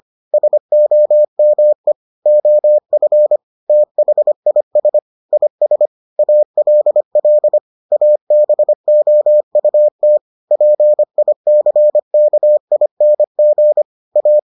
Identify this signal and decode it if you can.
Morse code, heard as a keyed tone, about 25 wpm, 600 Hz. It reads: SOME OF THIS IS ALL ABOUT PICKING A